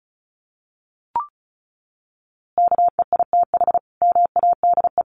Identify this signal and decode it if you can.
Morse code, heard as a keyed tone, about 34 wpm, 700 Hz. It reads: KEITH MADE
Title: A Sound Of Thunder 35wpm